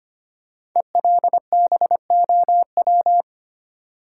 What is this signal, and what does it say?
Morse code, heard as a keyed tone, about 25 wpm, 700 Hz. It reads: ELBOW
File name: 5 Letter Words - First 3 Letters Given - 25wpm